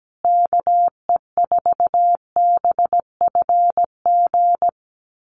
Morse code, heard as a keyed tone, about 17 wpm, 700 Hz. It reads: KE4BFG
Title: USA Call Signs Repeated 3x 17wpm